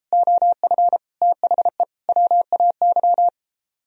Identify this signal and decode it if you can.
Morse code, heard as a keyed tone, about 33 wpm, 700 Hz. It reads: OF THE WAY